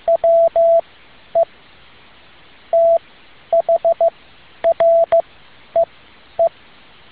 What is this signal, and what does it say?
Morse code, heard as a keyed tone, about 15 wpm, spaced as for 10 wpm, 670 Hz. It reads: WE THREE
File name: tin-upgrade / QSO 2013 02 12